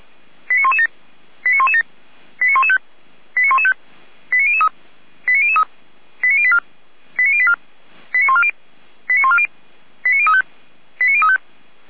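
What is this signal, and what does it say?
Five-tone selective call (ZVEI), heard as quick runs of five tones, tones of about 70 ms. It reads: 89117 89117 89115 89115 89003 89003 89084 89084 89140 89140 89035 89035
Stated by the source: Five-tone ZVEI selective call received on an SDR.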